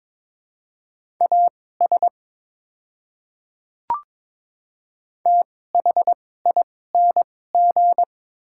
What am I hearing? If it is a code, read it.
Morse code, heard as a keyed tone, about 22 wpm, spaced as for 16 wpm, 700 Hz. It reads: AS THING
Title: 2x Character Spacing - Top 100 Words - 22wpm